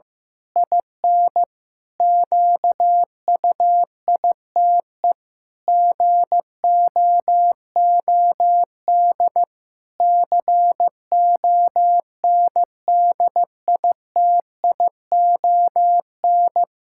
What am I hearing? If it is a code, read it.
Morse code, heard as a keyed tone, about 15 wpm, 700 Hz. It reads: IN QUITE GOOD CONDITION
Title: Pangram Challenge v2 - 15wpm